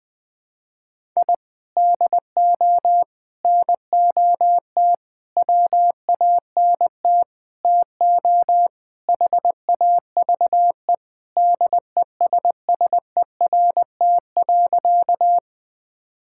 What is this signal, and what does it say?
Morse code, heard as a keyed tone, about 20 wpm, 700 Hz. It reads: I DO NOT WANT TO HAVE DESSERT.
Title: Sentences with 5-Word Phrases - Spoken Beginning - 20wpm